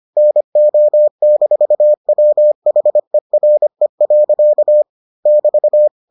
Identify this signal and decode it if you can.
Morse code, heard as a keyed tone, about 25 wpm, 600 Hz. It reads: NO-WHERE. <BT>